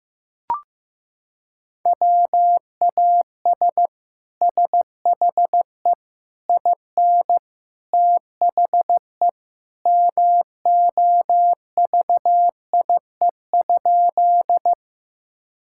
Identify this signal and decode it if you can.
Morse code, heard as a keyed tone, about 15 wpm, 700 Hz. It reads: WAS SHE IN THE MOVIE?
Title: Sentences from Top 1000 Words - Review Part 5 - 15wpm